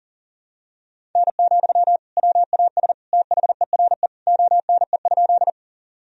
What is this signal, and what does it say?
Morse code, heard as a keyed tone, about 40 wpm, 700 Hz. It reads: N, WAS THERE ONE?
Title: Sentences from Top 100 Words and Abbreviations 40wpm